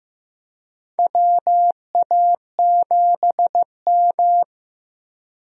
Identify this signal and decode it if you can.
Morse code, heard as a keyed tone, about 16 wpm, 700 Hz. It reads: WA7M
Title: USA Call Signs - 4 Characters - Repeated 2x - 15wpm